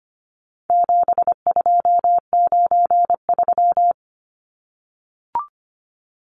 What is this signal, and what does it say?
Morse code, heard as a keyed tone, about 25 wpm, 700 Hz. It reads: 7293
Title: Numbers - 4-Digit 25wpm